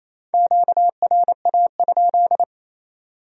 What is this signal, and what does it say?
Morse code, heard as a keyed tone, about 35 wpm, 700 Hz. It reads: QRA?